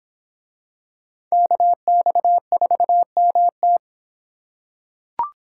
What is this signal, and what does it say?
Morse code, heard as a keyed tone, about 26 wpm, 700 Hz. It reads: KX4MT